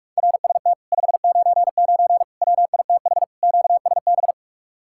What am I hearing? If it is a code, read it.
Morse code, heard as a keyed tone, about 44 wpm, 700 Hz. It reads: RST 599 WITH QSB